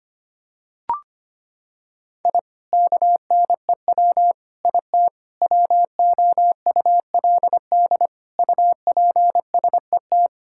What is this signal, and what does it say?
Morse code, heard as a keyed tone, about 25 wpm, 700 Hz. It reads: I KNEW IT WOULD UPSET
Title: Sentences from Top 1000 Words - Review Part 5 - 25wpm